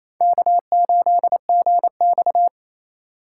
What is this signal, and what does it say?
Morse code, heard as a keyed tone, about 28 wpm, 700 Hz. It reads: K8GX
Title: USA Call Signs - 4 Characters - 3x Character Spacing - 28wpm